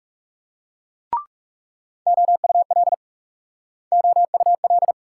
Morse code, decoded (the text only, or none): OUL OUL